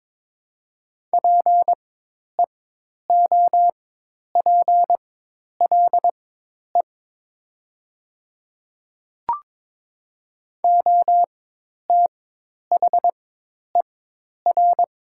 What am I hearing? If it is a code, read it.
Morse code, heard as a keyed tone, about 22 wpm, spaced as for 10 wpm, 700 Hz. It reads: PEOPLE OTHER
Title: 4x Character Spacing - Top 100 Words - 22wpm